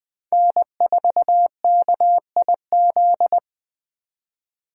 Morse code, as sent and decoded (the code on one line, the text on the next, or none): -. ....- -.- .. --..
N4KIZ